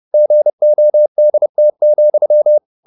Morse code, heard as a keyed tone, about 30 wpm, 600 Hz. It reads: GODT,
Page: Ferie 30 wpm